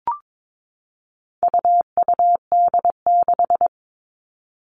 Morse code, heard as a keyed tone, about 22 wpm, 700 Hz. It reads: UUD6
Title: All Cut Numbers - 4-Digit 22wpm